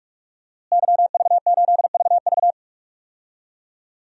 Morse code, heard as a keyed tone, about 46 wpm, 700 Hz. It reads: YV8VV
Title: DXCC-100 Call Signs - 5 Characters - Repeated 2x - 45wpm